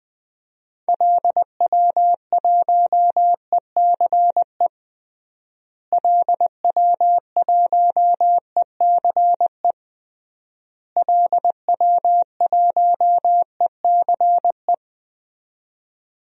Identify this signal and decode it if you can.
Morse code, heard as a keyed tone, about 20 wpm, 700 Hz. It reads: LW1ECE LW1ECE LW1ECE
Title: DXCC-100 Call Signs - 6 Characters - Repeated 3x - 20wpm